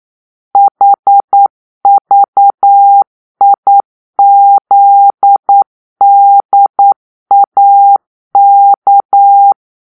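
Morse code, heard as a keyed tone, about 9 wpm, 800 Hz. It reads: HVIZDAK
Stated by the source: Morse-code cipher made by a player team for the Kachna puzzle game.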